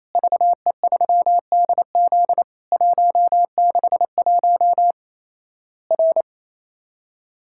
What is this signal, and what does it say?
Morse code, heard as a keyed tone, about 28 wpm, 700 Hz. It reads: VE3DZ 161 R
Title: Worked All Europe DX Contest - QTC Exchange - 28wpm